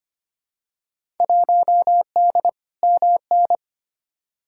Morse code, heard as a keyed tone, about 25 wpm, 700 Hz. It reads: 1D MN
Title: Field Day Exchanges - 25wpm